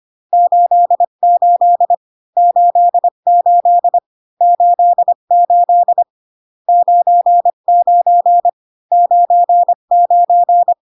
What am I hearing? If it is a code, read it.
Morse code, heard as a keyed tone, about 25 wpm, 700 Hz. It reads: 88 88 88 99 99